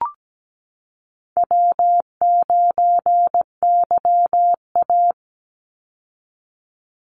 Morse code, heard as a keyed tone, about 17 wpm, 700 Hz. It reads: W9YA
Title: Call Signs Repeated 2x - 17wpm